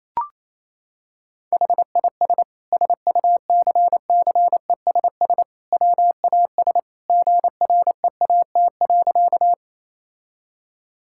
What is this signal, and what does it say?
Morse code, heard as a keyed tone, about 28 wpm, 700 Hz. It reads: HIS SUCCESS WAS GREAT.
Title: Sentences from Top 1000 Words - Review Part 6 - 28wpm